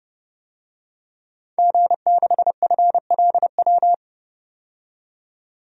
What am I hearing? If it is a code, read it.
Morse code, heard as a keyed tone, about 30 wpm, 700 Hz. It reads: G6FLW
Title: DXCC-100 Call Signs - 5 Characters - 30wpm